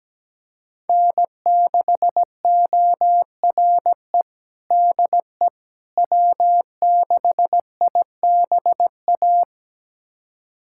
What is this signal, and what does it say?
Morse code, heard as a keyed tone, about 17 wpm, 700 Hz. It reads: N6ORE DE W6IBA
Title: USA Callsign DE Callsign - 5 Characters - 17wpm